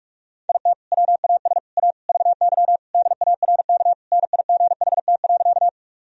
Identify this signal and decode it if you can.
Morse code, heard as a keyed tone, about 46 wpm, 690 Hz. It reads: IT WAS A VY DARK NIGHT.
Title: Sentences from Top 500 Words and Abbreviations 45wpm